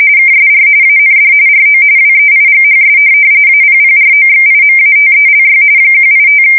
RTTY, about 45.45 baud. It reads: KSR AND RO WEIGHS APPROXIMATELY 185
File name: fsk170 / teletype-corp